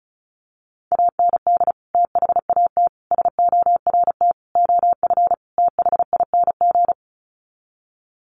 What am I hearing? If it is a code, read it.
Morse code, heard as a keyed tone, about 34 wpm, 700 Hz. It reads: AND THAT SORT OF THING
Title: Sets of 5 Words Encore2 35wpm